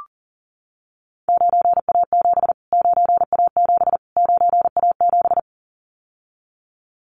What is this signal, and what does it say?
Morse code, heard as a keyed tone, about 40 wpm, 700 Hz. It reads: 9A7 9A7 9A7
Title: Callsign Prefixes for Top 25 DXCC - Repeated 3x - 40wpm